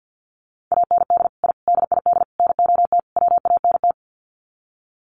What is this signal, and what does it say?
Morse code, heard as a keyed tone, about 50 wpm, 700 Hz. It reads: AND I DID NOT WANT
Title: Sets of 5 Words 50wpm